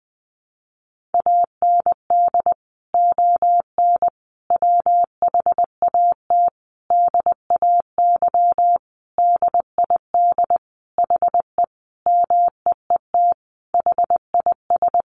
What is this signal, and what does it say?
Morse code, heard as a keyed tone, about 20 wpm, 700 Hz. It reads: AND ON WHAT DAY DID HE MEET HIS
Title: Sentences from Top 1100 Words 20wpm